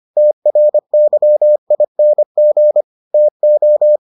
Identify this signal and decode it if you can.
Morse code, heard as a keyed tone, about 25 wpm, 600 Hz. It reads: TRYING TO